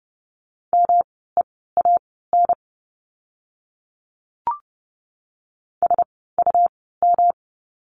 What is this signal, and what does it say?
Morse code, heard as a keyed tone, about 30 wpm, spaced as for 17 wpm, 700 Hz. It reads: MEAN SUM
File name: course / 30:10wpm / 108 Lesson 18f - Letter m - Words 30wpm